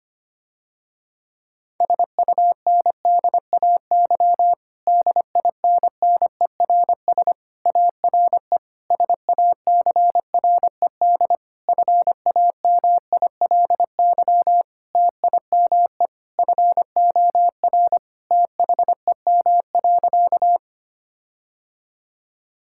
Morse code, heard as a keyed tone, about 25 wpm, 700 Hz. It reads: SUNDAY DINNERS ARE SACRED FAMILY TIME FOR THEM.
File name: practice / 25wpm / Difficult Sentences with Hints - 25wpm